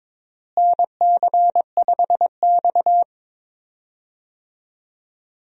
Morse code, heard as a keyed tone, about 22 wpm, 700 Hz. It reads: NC5X
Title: USA Call Signs - 4 Characters - Repeated 3x - 22wpm